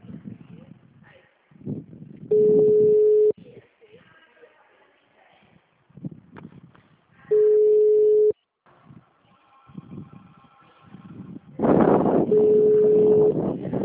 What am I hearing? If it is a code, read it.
Ringback tone (Europe, 425 Hz), with speech.
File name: out- -2015- - - .14903